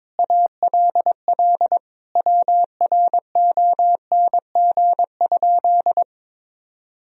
Morse code, heard as a keyed tone, about 22 wpm, 700 Hz. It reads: ALL WRONG?
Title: Sentences from Top 1000 Words - Review Part 1 - 22wpm